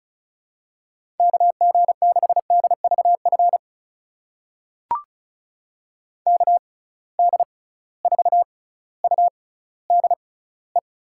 Morse code, heard as a keyed tone, about 36 wpm, spaced as for 18 wpm, 700 Hz. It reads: KG6DVF KD4UDE
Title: USA Call Signs - 6 Characters - 6x Character Spacing - 35wpm